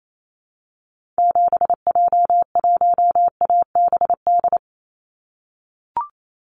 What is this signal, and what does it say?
Morse code, heard as a keyed tone, about 28 wpm, 700 Hz. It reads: 7J1ABD
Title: Call Signs - 28wpm